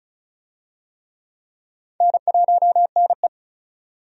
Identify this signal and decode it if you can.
Morse code, heard as a keyed tone, about 36 wpm, 700 Hz. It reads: N1NE
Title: USA Call Signs - 4 Characters - 35wpm